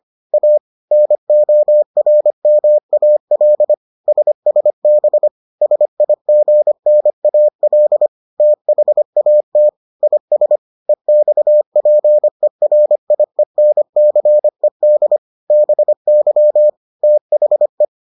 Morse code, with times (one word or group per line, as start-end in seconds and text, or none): 0.3-0.6 A
0.9-3.7 NORMAL
4.1-5.3 SSB
5.6-8.1 SIGNAL
8.4-9.7 THAT
10.0-10.6 IS
10.9-15.2 EXPERIENCED
15.5-16.7 BY
17.0-17.8 THE